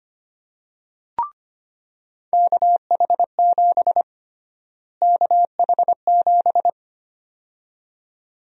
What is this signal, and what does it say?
Morse code, heard as a keyed tone, about 25 wpm, 700 Hz. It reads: KH7 KH7
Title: USA Call Sign Prefixes - Repeated 2x - 25wpm